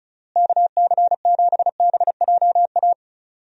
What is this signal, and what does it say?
Morse code, heard as a keyed tone, about 36 wpm, 700 Hz. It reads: KC7BJA
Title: USA Call Signs - 6 Characters - Repeated 3x - 35wpm